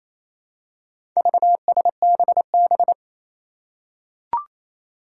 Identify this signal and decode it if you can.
Morse code, heard as a keyed tone, about 28 wpm, 700 Hz. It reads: VSBB
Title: Four Challenging Letter-Number Combinations 28wpm